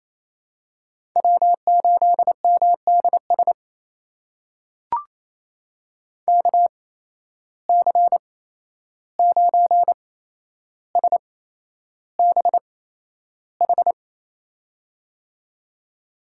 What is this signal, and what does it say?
Morse code, heard as a keyed tone, about 28 wpm, spaced as for 11 wpm, 700 Hz. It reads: W8MDS KC9SBH
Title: USA Call Signs - 8x Character Spacing - 28wpm